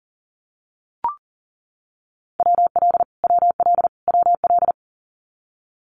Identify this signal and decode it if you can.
Morse code, heard as a keyed tone, about 40 wpm, 700 Hz. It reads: WL WL WL
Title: Abbreviations Repeated 3x - 40wpm